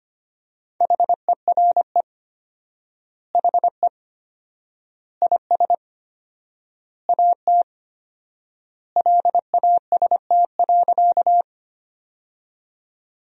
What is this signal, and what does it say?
Morse code, heard as a keyed tone, about 25 wpm, 700 Hz. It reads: HERE HE IS AT LAST.
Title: 4x Word-Spacing - Sentences from Top 300 Words - 25wpm